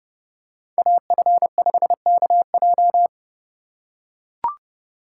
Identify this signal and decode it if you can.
Morse code, heard as a keyed tone, about 30 wpm, 700 Hz. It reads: AF5KJ
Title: USA Call Signs - 3x Character Spacing - 30wpm